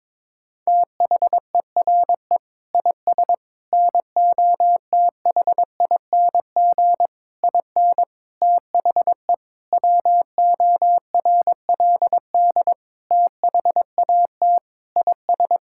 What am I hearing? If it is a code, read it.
Morse code, heard as a keyed tone, about 22 wpm, 700 Hz. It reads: THERE IS NOTHING IN THE WORLD THAT IS
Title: Sentences from Top 1600 Words 22wpm